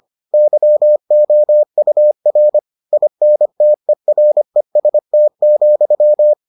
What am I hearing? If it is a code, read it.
Morse code, heard as a keyed tone, about 25 wpm, 600 Hz. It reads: YOUR INTEREST,